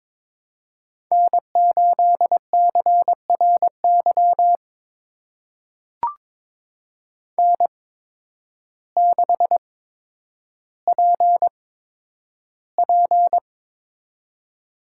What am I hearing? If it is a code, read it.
Morse code, heard as a keyed tone, about 23 wpm, spaced as for 10 wpm, 700 Hz. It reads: N8CRY N6PP
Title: USA Call Signs - 8x Character Spacing - 22wpm